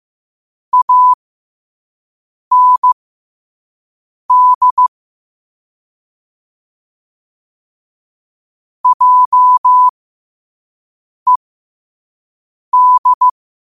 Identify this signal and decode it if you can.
Morse code, heard as a keyed tone, about 15 wpm, spaced as for 5 wpm, 1000 Hz. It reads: AND JED
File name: qotdfiles / 5wpm